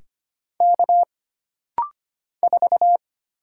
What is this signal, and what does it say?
Morse code, heard as a keyed tone, about 25 wpm, 700 Hz. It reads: K 4
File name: Single Character (Letter:Number:Punctuation:Prosign) - Rapid-Fire - 25wpm